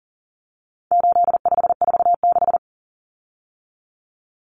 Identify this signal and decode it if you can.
Morse code, heard as a keyed tone, about 40 wpm, 700 Hz. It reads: 8546
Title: Common Cut Numbers - 4-Digit 40wpm